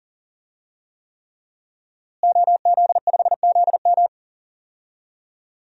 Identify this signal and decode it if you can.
Morse code, heard as a keyed tone, about 39 wpm, 700 Hz. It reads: OZ5ZM
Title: DXCC-100 Call Signs - 5 Characters - Repeated 3x - 40wpm